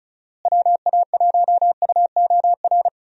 Morse code, heard as a keyed tone, about 36 wpm, 700 Hz. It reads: WA1UOR
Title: USA Call Signs - 6 Characters - Repeated 3x - 35wpm